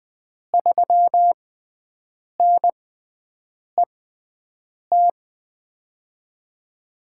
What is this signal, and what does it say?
Morse code, heard as a keyed tone, about 20 wpm, spaced as for 7 wpm, 700 Hz. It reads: 3NET
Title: USA Call Signs - 6 Characters - 6x Character Spacing - 20wpm